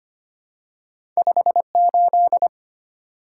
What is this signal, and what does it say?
Morse code, heard as a keyed tone, about 25 wpm, 700 Hz. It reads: 58
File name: Two Letter-Number Combinations - 25wpm